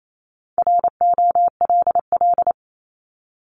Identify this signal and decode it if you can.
Morse code, heard as a keyed tone, about 28 wpm, 700 Hz. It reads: ROLL
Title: Multi-letter Phonemes 28wpm